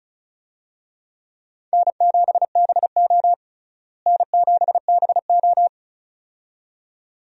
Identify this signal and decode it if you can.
Morse code, heard as a keyed tone, about 36 wpm, 700 Hz. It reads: N7BO N7BO
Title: USA Call Signs - 4 Characters - Repeated 2x - 35wpm